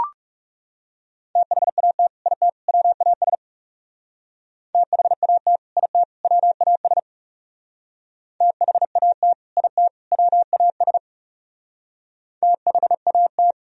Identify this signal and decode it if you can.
Morse code, heard as a keyed tone, about 40 wpm, 700 Hz. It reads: THAT IT WAS THAT IT WAS THAT IT WAS THAT